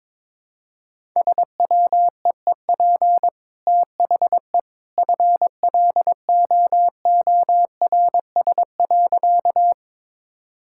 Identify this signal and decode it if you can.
Morse code, heard as a keyed tone, about 22 wpm, 700 Hz. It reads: SWEEP THE FLOORS.
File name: practice / 22wpm / Easy Sentences with Hints - 22wpm